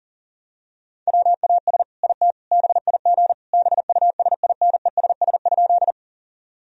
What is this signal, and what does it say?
Morse code, heard as a keyed tone, about 40 wpm, 700 Hz. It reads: WAS IT BIG BUSINESS?
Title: Sentences from Top 1000 Words - Review Part 1 - 40wpm